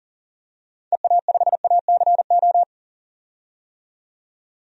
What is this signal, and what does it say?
Morse code, heard as a keyed tone, about 37 wpm, 700 Hz. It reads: EA5ACO
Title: DXCC-100 Call Signs - 6 Characters - 40wpm